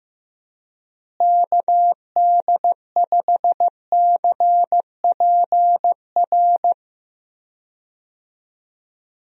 Morse code, heard as a keyed tone, about 15 wpm, 700 Hz. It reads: KD5CPR
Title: USA Call Signs - 6 Characters - Repeated 2x - 15wpm